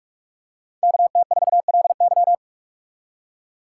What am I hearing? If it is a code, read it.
Morse code, heard as a keyed tone, about 44 wpm, 700 Hz. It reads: KT4LY